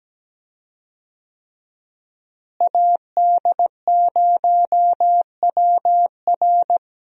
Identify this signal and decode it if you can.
Morse code, heard as a keyed tone, about 17 wpm, 700 Hz. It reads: AD0WR
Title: USA Call Signs - 5 Characters - Repeated 3x - 17wpm